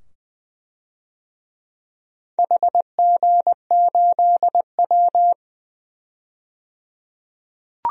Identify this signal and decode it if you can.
Morse code, heard as a keyed tone, about 20 wpm, 700 Hz. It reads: HG8W
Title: DXCC-100 Call Signs - 4 Characters - 20wpm